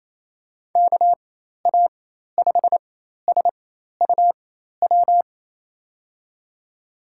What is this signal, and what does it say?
Morse code, heard as a keyed tone, about 28 wpm, spaced as for 13 wpm, 700 Hz. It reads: KA5SUW
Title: USA Call Signs - 6 Characters - 4x Character Spacing - 28wpm